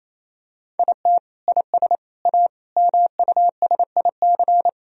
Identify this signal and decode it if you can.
Morse code, heard as a keyed tone, about 28 wpm, 700 Hz. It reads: IT IS A MUSIC
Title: Sentences from Top 1000 Words - Review Part 4 - 28wpm